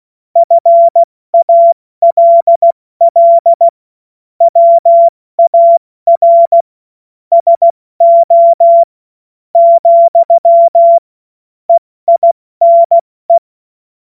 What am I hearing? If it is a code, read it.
Morse code, heard as a keyed tone, about 16 wpm, 680 Hz. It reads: FALL WAR SO , EINE
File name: Morsekurs / Witze / Witz 02 April 2024 70 F80